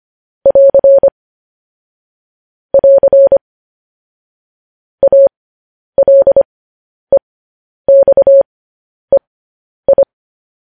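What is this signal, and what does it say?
Morse code, heard as a keyed tone, about 25 wpm, spaced as for 10 wpm, 550 Hz. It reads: <AR> <AR> ALEXEI
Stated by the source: Morse code news headlines sent at 10 words per minute.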